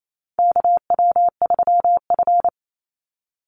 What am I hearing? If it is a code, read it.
Morse code, heard as a keyed tone, about 28 wpm, 700 Hz. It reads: KW3F
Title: Call Signs - 28wpm